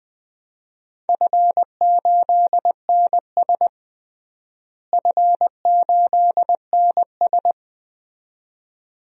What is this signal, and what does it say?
Morse code, heard as a keyed tone, about 21 wpm, 700 Hz. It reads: F8NS F8NS
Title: DXCC-100 Call Signs - 4 Characters - Repeated 2x - 20wpm